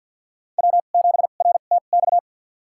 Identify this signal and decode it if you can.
Morse code, heard as a keyed tone, about 47 wpm, spaced as for 35 wpm, 700 Hz. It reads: W7RTX
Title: USA Call Signs - 5 Characters - 2x Character Spacing - 50wpm